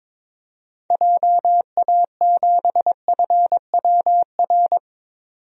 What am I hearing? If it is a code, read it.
Morse code, heard as a keyed tone, about 22 wpm, 700 Hz. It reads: JA7FWR
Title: DXCC-100 Call Signs - 6 Characters - 22wpm